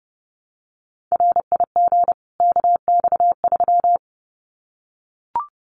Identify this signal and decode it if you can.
Morse code, heard as a keyed tone, about 29 wpm, 700 Hz. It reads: RIG KX3
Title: Common QSO Phrases 30wpm